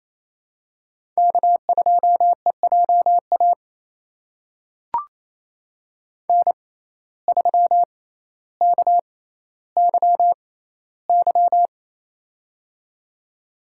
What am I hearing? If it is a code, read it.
Morse code, heard as a keyed tone, about 28 wpm, spaced as for 14 wpm, 700 Hz. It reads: K2EJA N3KYY
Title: USA Call Signs - 5 Characters - 6x Character Spacing - 28wpm